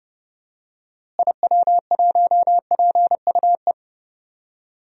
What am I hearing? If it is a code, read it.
Morse code, heard as a keyed tone, about 28 wpm, 700 Hz. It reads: IW1PUE
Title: DXCC-100 Call Signs - 6 Characters - Repeated 3x - 30wpm